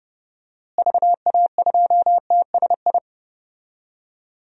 Morse code, heard as a keyed tone, about 30 wpm, 700 Hz. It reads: VA2TSI
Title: DXCC-100 Call Signs - 6 Characters - Repeated 2x - 30wpm